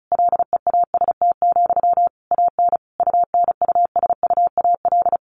Morse code, heard as a keyed tone, about 34 wpm, 700 Hz. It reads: LEAST, AN UNUSUAL
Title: Sentences from Top 1400 Words 35wpm